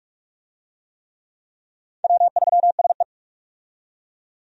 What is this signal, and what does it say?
Morse code, heard as a keyed tone, about 45 wpm, 700 Hz. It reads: W3SE
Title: USA Call Signs - 4 Characters - 45wpm